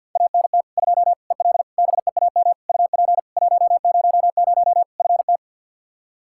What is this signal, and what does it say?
Morse code, heard as a keyed tone, about 52 wpm, 700 Hz. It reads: ANT 2 EL BEAM UP 100 FT
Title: Practice QSO - Exchange 2 - Info - 50wpm